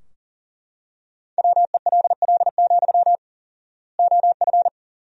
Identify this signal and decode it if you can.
Morse code, heard as a keyed tone, about 40 wpm, 700 Hz. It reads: WELL, OF
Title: 4x Word-Spacing - Sentences from Top 500 Words - 40wpm